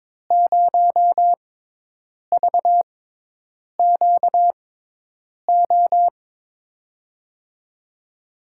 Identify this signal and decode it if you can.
Morse code, heard as a keyed tone, about 21 wpm, spaced as for 8 wpm, 700 Hz. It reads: 0VQO
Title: USA Call Signs - 6 Characters - 6x Character Spacing - 22wpm